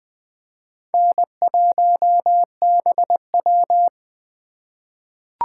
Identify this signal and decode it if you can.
Morse code, heard as a keyed tone, about 20 wpm, 700 Hz. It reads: N1BW
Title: USA Call Signs - 4 Characters - 2x Character Spacing - 20wpm